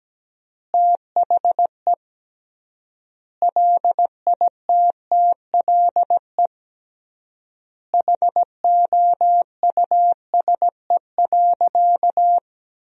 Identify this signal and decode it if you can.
Morse code, heard as a keyed tone, about 17 wpm, 700 Hz. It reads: THE LITTLE HOUSE.
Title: 3x Word-Spacing - Sentences from Top 200 Words - 17wpm